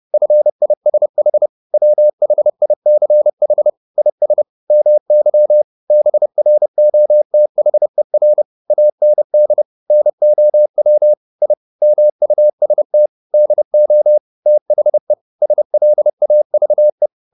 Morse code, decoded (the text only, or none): FISH WHICH IS MY BROTHER AND NOW I MUST DO THE SLAVE